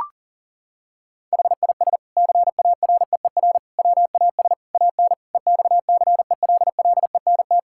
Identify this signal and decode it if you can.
Morse code, heard as a keyed tone, about 40 wpm, 700 Hz. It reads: HIS CAREER WAS AN EXCELLENT
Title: Sentences from Top 1000 Words - Review Part 6 - 40wpm